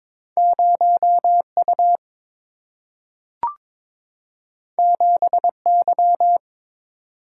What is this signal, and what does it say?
Morse code, heard as a keyed tone, about 23 wpm, 700 Hz. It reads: 0U 7Y
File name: practice / 22wpm / Two Letter-Number Combinations - 22wpm